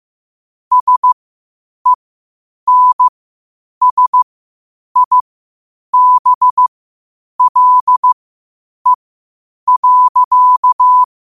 Morse code, heard as a keyed tone, about 15 wpm, spaced as for 8 wpm, 1000 Hz. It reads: SENSIBLE.
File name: qotdfiles / 8wpm